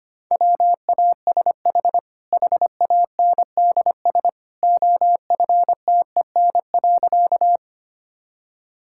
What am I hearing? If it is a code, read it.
Morse code, heard as a keyed tone, about 25 wpm, 690 Hz. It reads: WASH HANDS OFTEN.
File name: Easy Sentences with Hints - 25wpm